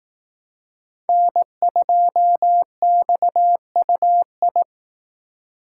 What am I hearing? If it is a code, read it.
Morse code, heard as a keyed tone, about 19 wpm, 700 Hz. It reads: N2XUI
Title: Speed Racing - USA 1x3 Call Signs 15wpm